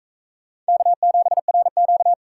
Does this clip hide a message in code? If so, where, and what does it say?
Yes: Morse code from 0.7 to 2.1 s, heard as a keyed tone, about 41 wpm, 700 Hz. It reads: K7RQ